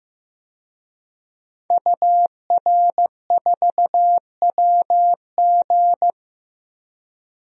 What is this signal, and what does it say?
Morse code, heard as a keyed tone, about 16 wpm, 700 Hz. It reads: UR4WG
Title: DXCC-100 Call Signs - 5 Characters - 15wpm